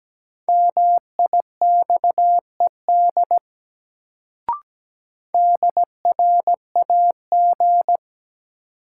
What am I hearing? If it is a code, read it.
Morse code, heard as a keyed tone, about 17 wpm, 700 Hz. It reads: MIXED DRAG
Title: Sentences from Top 1500 Words - New Word Drill - 17wpm